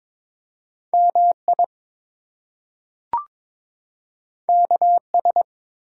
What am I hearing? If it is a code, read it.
Morse code, heard as a keyed tone, about 22 wpm, 700 Hz. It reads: MI KS
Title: US States and Canadian Provinces 22wpm